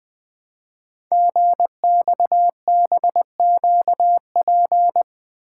Morse code, heard as a keyed tone, about 20 wpm, 700 Hz. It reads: GXBQP
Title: Code Groups - 5 Letters - 20wpm